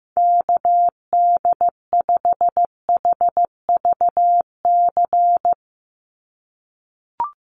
Morse code, heard as a keyed tone, about 15 wpm, 700 Hz. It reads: KD5HVC
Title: USA Call Signs 15wpm